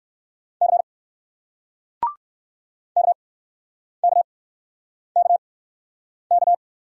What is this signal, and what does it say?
Morse code, heard as a keyed tone, about 60 wpm, spaced as for 9 wpm, 700 Hz. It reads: C KKKK